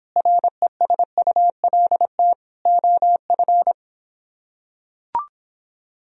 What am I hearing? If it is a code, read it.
Morse code, heard as a keyed tone, about 24 wpm, 700 Hz. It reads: RESULT OF